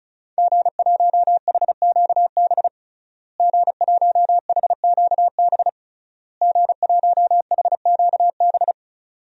Morse code, heard as a keyed tone, about 36 wpm, 700 Hz. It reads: G1HQB G1HQB G1HQB
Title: DXCC-100 Call Signs - 5 Characters - Repeated 3x - 35wpm